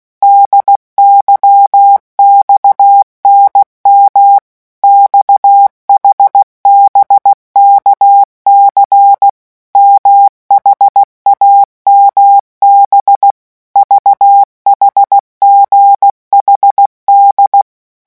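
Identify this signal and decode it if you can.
Morse code, heard as a keyed tone, about 16 wpm, 800 Hz. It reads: DYXNM XHBKC MHAMB VHGHD
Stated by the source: Morse letter-group revision drill at 15 words per minute.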